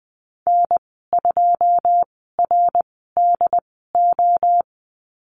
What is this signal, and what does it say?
Morse code, heard as a keyed tone, about 20 wpm, spaced as for 14 wpm, 700 Hz. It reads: N2RDO
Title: 207 Lesson 33g - Letter j - Call Signs 20wpm